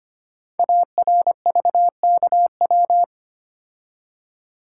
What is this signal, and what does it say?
Morse code, heard as a keyed tone, about 25 wpm, 700 Hz. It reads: ARVKW
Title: Code Groups - 5 Letters - 25wpm